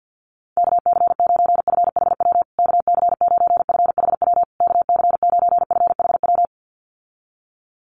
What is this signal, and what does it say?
Morse code, heard as a keyed tone, about 49 wpm, 700 Hz. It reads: KC9FHW KC9FHW KC9FHW
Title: USA Call Signs Repeated 3x 50wpm